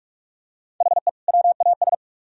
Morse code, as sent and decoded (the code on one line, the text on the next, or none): .... / .-- .- ...
HE WAS